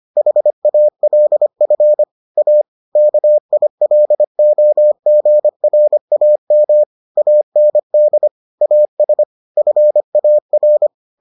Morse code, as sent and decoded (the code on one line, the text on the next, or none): .... .- .-.. ..-. / .- / -.- .. .-.. --- --. .-. .- -- / .- -. -.. / .- ... / ..-. .- .-.
HALF A KILOGRAM AND AS FAR